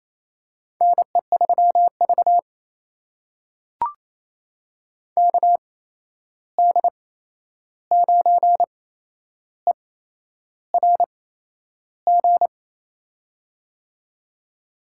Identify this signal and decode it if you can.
Morse code, heard as a keyed tone, about 28 wpm, spaced as for 11 wpm, 700 Hz. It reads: NE3V KD9ERG
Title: USA Call Signs - 8x Character Spacing - 28wpm